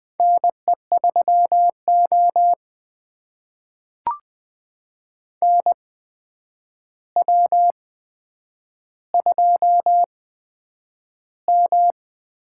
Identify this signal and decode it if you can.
Morse code, heard as a keyed tone, about 20 wpm, spaced as for 9 wpm, 700 Hz. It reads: NE3O NW2M
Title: USA Call Signs - 4 Characters - 8x Character Spacing - 20wpm